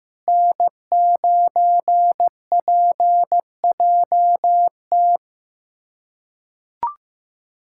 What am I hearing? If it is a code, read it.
Morse code, heard as a keyed tone, about 15 wpm, 700 Hz. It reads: N9PJT
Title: USA Callsign DE Callsign - 5 Characters - 15wpm